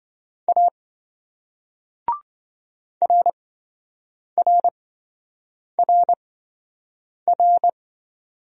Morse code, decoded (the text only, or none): A RRRR